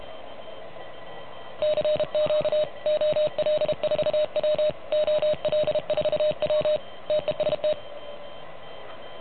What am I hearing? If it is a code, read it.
Morse code, heard as a keyed tone, about 33 wpm, 610 Hz. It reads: CQ OL4W OL4W TEST
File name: truSDX / MWC29.8